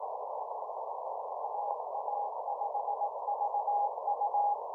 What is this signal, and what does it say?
Morse code, heard as a keyed tone, about 18 wpm, 820 Hz. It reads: AA5VU
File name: simple-file-list / k5d 20m cw